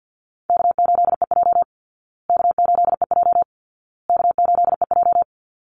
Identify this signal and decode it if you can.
Morse code, heard as a keyed tone, about 50 wpm, 700 Hz. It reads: K8EJ K8EJ K8EJ
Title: Call Signs Repeated 2x - 50wpm